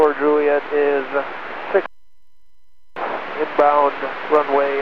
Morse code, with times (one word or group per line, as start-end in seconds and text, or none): none